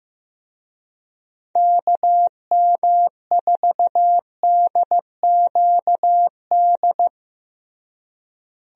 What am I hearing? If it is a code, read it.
Morse code, heard as a keyed tone, about 15 wpm, 700 Hz. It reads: KM4DQD
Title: USA Call Signs - 6 Characters - Repeated 3x - 15wpm